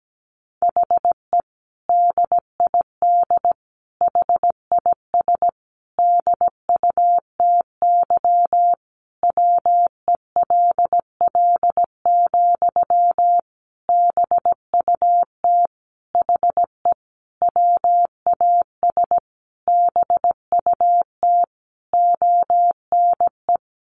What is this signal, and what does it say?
Morse code, heard as a keyed tone, about 17 wpm, 700 Hz. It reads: HE DID HIS DUTY WELL, BUT HE WAS BUT ONE
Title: Sentences from Top 1500 Words 17wpm